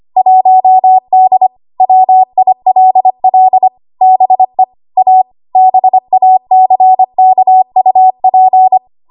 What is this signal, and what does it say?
Morse code, heard as a keyed tone, about 25 wpm, 750 Hz. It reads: 1D WILL BE A BACKUP
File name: Archive / 25 WPM / 25WPM